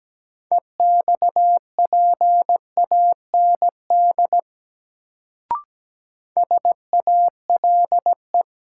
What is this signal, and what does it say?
Morse code, heard as a keyed tone, about 17 wpm, 700 Hz. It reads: EXPAND SALE
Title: Sentences from Top 1400 Words - New Word Drill - 17wpm